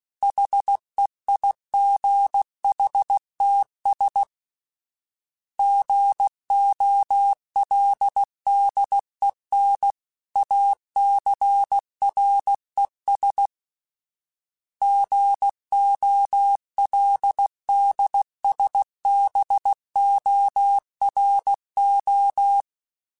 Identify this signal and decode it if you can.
Morse code, heard as a keyed tone, about 16 wpm, 800 Hz. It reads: HEIGHTS GOLDEN ACRES GOLDSBORO
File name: fromSMs / FromSMs mp3s / Cities@15